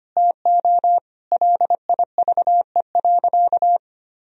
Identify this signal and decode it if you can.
Morse code, heard as a keyed tone, about 25 wpm, 700 Hz. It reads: TO LIVE.
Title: Sentences from Top 1000 Words - Review Part 3 - 25wpm